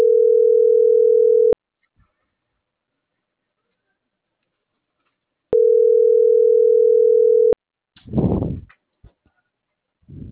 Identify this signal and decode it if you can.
Ringback tone (North America), 440 + 480 Hz.